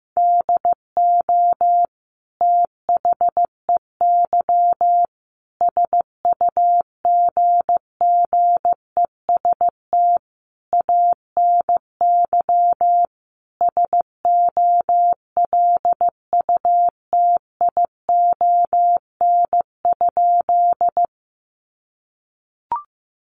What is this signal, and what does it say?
Morse code, heard as a keyed tone, about 15 wpm, 700 Hz. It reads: DO THEY SUGGEST ANY SOLUTION?
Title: Sentences from Top 1300 Words 15wpm